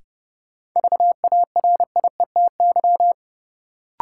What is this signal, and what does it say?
Morse code, heard as a keyed tone, about 30 wpm, 700 Hz. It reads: VARIETY
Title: Sentences from Top 1000 Words - Review Part 4 - New Word Drill - 30wpm